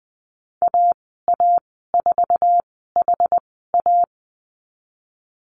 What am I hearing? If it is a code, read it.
Morse code, heard as a keyed tone, about 20 wpm, spaced as for 14 wpm, 700 Hz. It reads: AA4HA
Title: 069 Lesson 12g - Letter d - Call Signs 20wpm